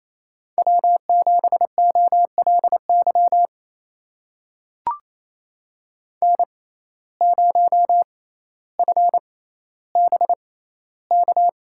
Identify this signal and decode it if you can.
Morse code, heard as a keyed tone, about 28 wpm, spaced as for 14 wpm, 700 Hz. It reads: W7OLY N0FBK
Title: USA Call Signs - 5 Characters - 6x Character Spacing - 28wpm